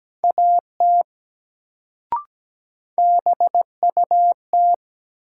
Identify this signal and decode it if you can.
Morse code, heard as a keyed tone, about 17 wpm, 700 Hz. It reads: AT BUT
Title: Sentences from Top 100 Words - New Word Drill - 17wpm